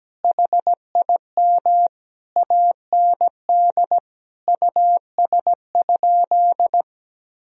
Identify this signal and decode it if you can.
Morse code, heard as a keyed tone, about 17 wpm, 700 Hz. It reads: HIM AND US?
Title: Sentences from Top 1000 Words - Review Part 3 - 17wpm